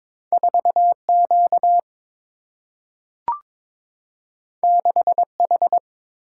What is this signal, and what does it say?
Morse code, heard as a keyed tone, about 22 wpm, 700 Hz. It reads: 4Q 6H
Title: Single-Letter Single-Number 22wpm